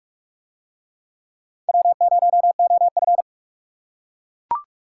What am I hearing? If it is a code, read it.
Morse code, heard as a keyed tone, about 38 wpm, 700 Hz. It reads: W0OF